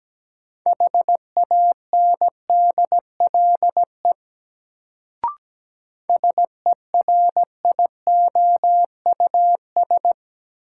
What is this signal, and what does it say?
Morse code, heard as a keyed tone, about 17 wpm, 700 Hz. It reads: HANDLE SERIOUS
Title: Sentences from Top 1100 Words - New Word Drill Alt - 17wpm